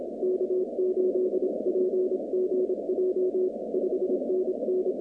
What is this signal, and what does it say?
Morse code, heard as a keyed tone, about 27 wpm, 370 Hz, with noise. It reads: K8PGJ 5NN
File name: 2019-cw-wpx-kt5j-k5pi